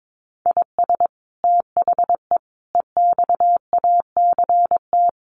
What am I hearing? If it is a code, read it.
Morse code, heard as a keyed tone, about 22 wpm, 700 Hz. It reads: IS THE EXACT
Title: Sentences from Top 1300 Words 22wpm